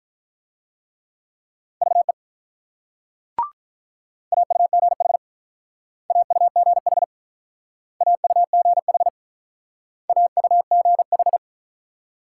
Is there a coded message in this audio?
Yes: Morse code from 1.8 to 11.4 s, heard as a keyed tone, about 47 wpm, 700 Hz. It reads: VE AUGH AUGH AUGH AUGH